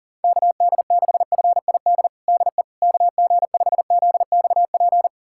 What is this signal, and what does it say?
Morse code, heard as a keyed tone, about 37 wpm, 700 Hz. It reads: KD6FID DE KG5ZXP
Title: USA Callsign DE Callsign - 6 Characters - 40wpm